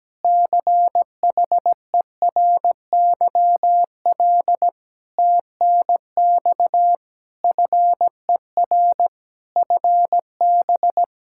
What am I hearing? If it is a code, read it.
Morse code, heard as a keyed tone, about 17 wpm, 700 Hz. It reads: CHERYL TNX FER FB
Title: Practice QSO - Exchange 4 - Ending - 17wpm